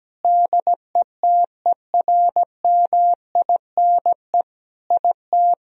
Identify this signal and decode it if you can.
Morse code, heard as a keyed tone, about 17 wpm, 700 Hz. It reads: DETERMINE IT
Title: Sentences from Top 1000 Words - Review Part 4 - 17wpm